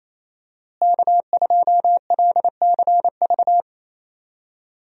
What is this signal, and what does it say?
Morse code, heard as a keyed tone, about 28 wpm, 700 Hz. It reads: K2LCV
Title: USA Call Signs - 5 Characters - Repeated 2x - 28wpm